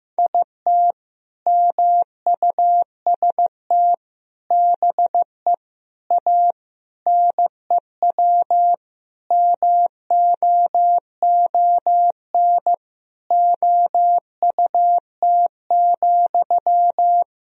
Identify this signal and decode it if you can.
Morse code, heard as a keyed tone, about 15 wpm, 700 Hz. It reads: IT MUST BE A NEW MOON OUT,
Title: Sentences from Top 500 Words and Abbreviations 15wpm